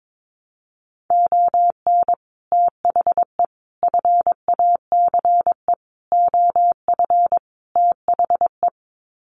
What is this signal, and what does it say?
Morse code, heard as a keyed tone, about 22 wpm, 700 Hz. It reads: ON THE FACE OF THE
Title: Sets of 5 Words Encore2 22wpm